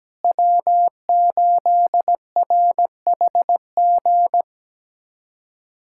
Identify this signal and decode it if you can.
Morse code, heard as a keyed tone, about 17 wpm, 700 Hz. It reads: W8RHG
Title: USA Callsign DE Callsign - 5 Characters - 17wpm